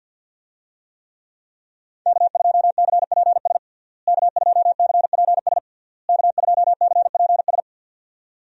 Morse code, heard as a keyed tone, about 47 wpm, 700 Hz. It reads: K2CPS K2CPS K2CPS
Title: USA Call Signs - 5 Characters - Repeated 3x - 50wpm